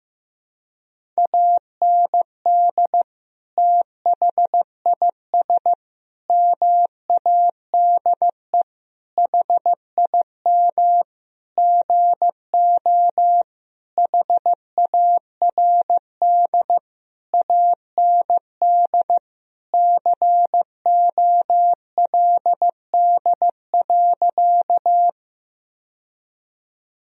Morse code, heard as a keyed tone, about 15 wpm, 700 Hz. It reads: AND THIS MADE HIM GO HARD AND COLD.
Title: Sentences from Top 1000 Words - Review Part 1 - 15wpm